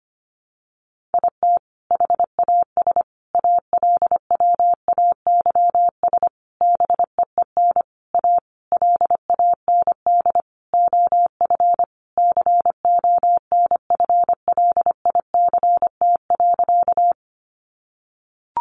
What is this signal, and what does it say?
Morse code, heard as a keyed tone, about 25 wpm, 700 Hz. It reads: IT HAS ALWAYS BEEN A LAND OF CONFLICT.
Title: Sentences from Top 1300 Words 25wpm